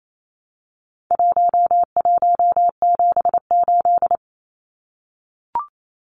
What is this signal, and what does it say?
Morse code, heard as a keyed tone, about 28 wpm, 700 Hz. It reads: 1178
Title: Numbers - 4-Digit 28wpm